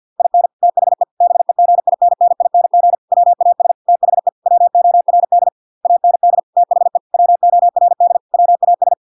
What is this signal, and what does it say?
Morse code, heard as a keyed tone, about 50 wpm, 700 Hz. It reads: IN THE BEGINNING WAS THE WORD AND THE WORD WAS